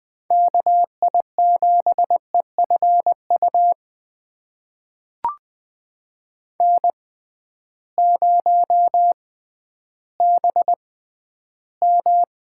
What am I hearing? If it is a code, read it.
Morse code, heard as a keyed tone, about 20 wpm, spaced as for 12 wpm, 700 Hz. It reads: KI7EFU N0BM
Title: USA Call Signs - 6x Character Spacing - 20wpm